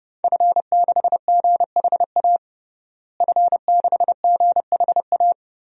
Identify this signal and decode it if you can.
Morse code, heard as a keyed tone, about 29 wpm, 700 Hz. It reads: F6GHA F6GHA
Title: DXCC-100 Call Signs - 5 Characters - Repeated 2x - 30wpm